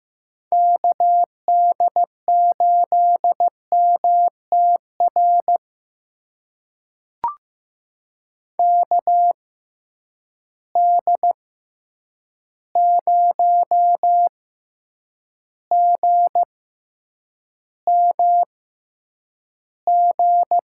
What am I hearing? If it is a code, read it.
Morse code, heard as a keyed tone, about 15 wpm, spaced as for 8 wpm, 700 Hz. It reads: KD8MTR KD0GMG
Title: USA Call Signs - 6 Characters - 6x Character Spacing - 15wpm